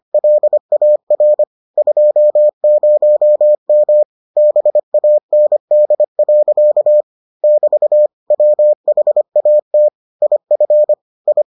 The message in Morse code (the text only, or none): LAR 20M BAND. <BT> WHAT IF I